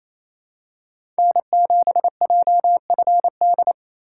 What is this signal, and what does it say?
Morse code, heard as a keyed tone, about 28 wpm, 700 Hz. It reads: N7JFD
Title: USA Call Signs - 5 Characters - 2x Character Spacing - 28wpm